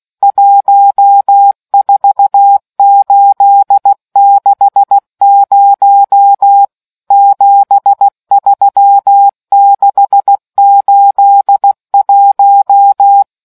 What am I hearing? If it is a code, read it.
Morse code, heard as a keyed tone, about 16 wpm, 800 Hz. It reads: 14860 73681